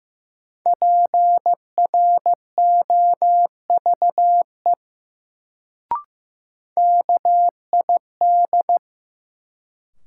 Morse code, heard as a keyed tone, about 15 wpm, 700 Hz. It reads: PROVE KID
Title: Sentences from Top 1600 Words - New Word Drill - 15wpm